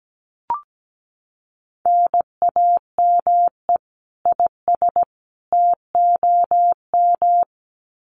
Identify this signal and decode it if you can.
Morse code, heard as a keyed tone, about 17 wpm, 700 Hz. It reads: NAME IS TOM
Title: Common QSO Phrases 17wpm